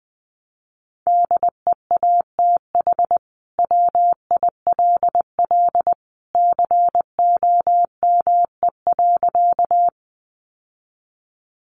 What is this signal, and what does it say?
Morse code, heard as a keyed tone, about 20 wpm, 700 Hz. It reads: DEATH WILL COME.
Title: Sentences from Top 1100 Words 20wpm